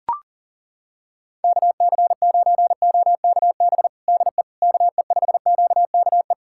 Morse code, heard as a keyed tone, about 40 wpm, 700 Hz. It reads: KC9OKB DE KE5QKE
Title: USA Callsign DE Callsign - 6 Characters - 40wpm